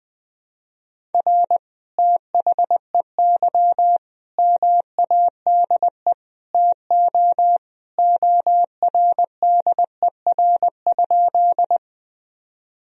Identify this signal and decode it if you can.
Morse code, heard as a keyed tone, about 21 wpm, 700 Hz. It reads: R THEY MADE TO ORDER?
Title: Sentences from Top 400 Words and Abbreviations 20wpm